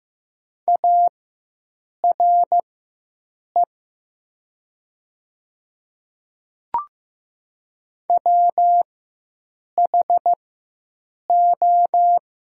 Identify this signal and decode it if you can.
Morse code, heard as a keyed tone, about 15 wpm, spaced as for 7 wpm, 700 Hz. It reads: ARE WHO
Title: 4x Character Spacing - Top 100 Words - 15wpm